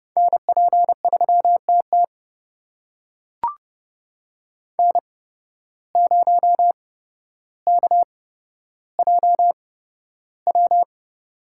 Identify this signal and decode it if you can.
Morse code, heard as a keyed tone, about 30 wpm, spaced as for 13 wpm, 700 Hz. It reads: NP3TT N0KJW
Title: USA Call Signs - 8x Character Spacing - 30wpm